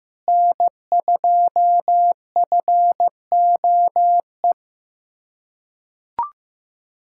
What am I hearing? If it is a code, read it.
Morse code, heard as a keyed tone, about 15 wpm, 700 Hz. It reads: N2FOE
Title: USA Callsign DE Callsign - 5 Characters - 15wpm